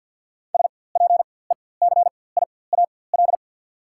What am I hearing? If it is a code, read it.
Morse code, heard as a keyed tone, about 47 wpm, spaced as for 23 wpm, 690 Hz. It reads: SPECIAL
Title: 4x Character Spacing - Top 500 Words - 50wpm